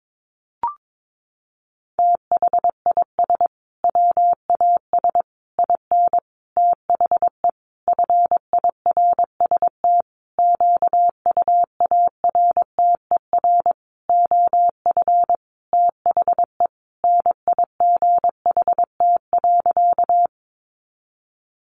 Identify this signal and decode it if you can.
Morse code, heard as a keyed tone, about 22 wpm, 700 Hz. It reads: THIS WAS IN THE FIRST QUARTER OF THE NIGHT.
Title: Sentences from Top 1300 Words 22wpm